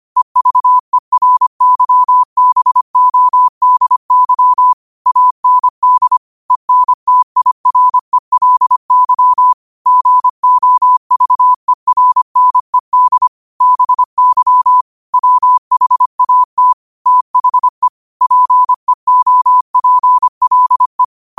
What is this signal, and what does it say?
Morse code, heard as a keyed tone, about 25 wpm, 1000 Hz. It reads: EVERYBODY AND ENTIRELY GOVERNED BY WHAT THE PEOPLE
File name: qotdfiles / 25wpm